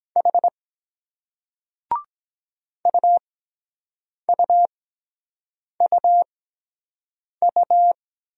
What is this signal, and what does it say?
Morse code, heard as a keyed tone, about 22 wpm, spaced as for 7 wpm, 700 Hz. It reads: H UUUU